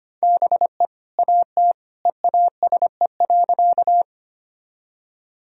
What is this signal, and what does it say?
Morse code, heard as a keyed tone, about 25 wpm, 700 Hz. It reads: BE AT EASE.
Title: Sentences from Top 1000 Words - Review Part 1 - 25wpm